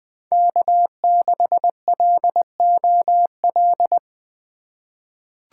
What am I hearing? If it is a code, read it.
Morse code, heard as a keyed tone, about 20 wpm, 700 Hz. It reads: K6LOL
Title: USA Call Signs - 5 Characters - 2x Character Spacing - 20wpm